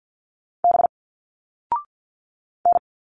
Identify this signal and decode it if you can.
Morse code, heard as a keyed tone, about 50 wpm, 700 Hz. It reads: B N
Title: Single Letters 50wpm